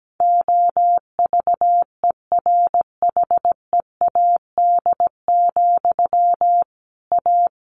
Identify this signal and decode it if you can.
Morse code, heard as a keyed tone, about 17 wpm, 700 Hz. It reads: OVERHEAD, A
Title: A Sound Of Thunder 17wpm